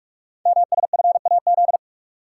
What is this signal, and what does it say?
Morse code, heard as a keyed tone, about 44 wpm, 700 Hz. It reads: MSFAZ
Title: Code Groups - 5 Letters - 45wpm